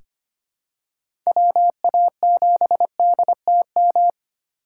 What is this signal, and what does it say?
Morse code, heard as a keyed tone, about 25 wpm, 700 Hz. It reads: WA7DTM
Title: USA Call Signs - 6 Characters - 4x Character Spacing - 25wpm